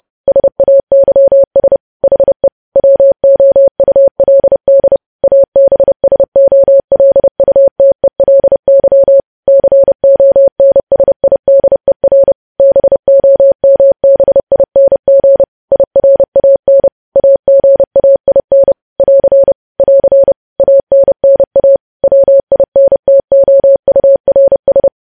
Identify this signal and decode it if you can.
Morse code, heard as a keyed tone, about 30 wpm, 550 Hz. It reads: SAYS HE WOULD ABSOLUTELY CONSIDER BOMBING IRAN AGAIN <AR> <AR> ANNA WINTOURS
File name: podcast / News headlines-1x-30x30